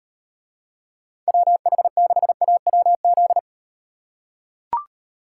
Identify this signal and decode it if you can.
Morse code, heard as a keyed tone, about 38 wpm, 700 Hz. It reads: WH6AWZ